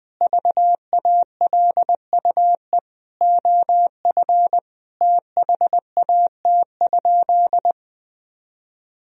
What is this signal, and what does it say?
Morse code, heard as a keyed tone, about 20 wpm, 700 Hz. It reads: VALUE OF THAT?
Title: Sentences from Top 1000 Words - Review Part 3 - 20wpm